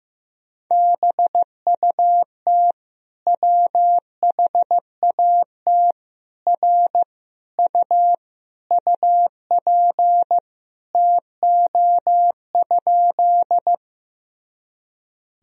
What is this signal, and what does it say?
Morse code, heard as a keyed tone, about 15 wpm, 700 Hz. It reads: BUT WHAT R U UP TO?
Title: Sentences from Top 100 Words and Abbreviations 15wpm